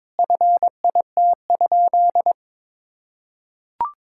Morse code, heard as a keyed tone, about 22 wpm, 700 Hz. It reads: FIT?
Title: Sentences from Top 1000 Words - Review Part 5 - 22wpm